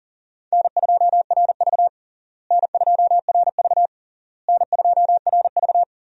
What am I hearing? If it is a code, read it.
Morse code, heard as a keyed tone, about 40 wpm, 700 Hz. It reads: N2RV N2RV N2RV
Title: USA Call Signs - 4 Characters - Repeated 3x - 40wpm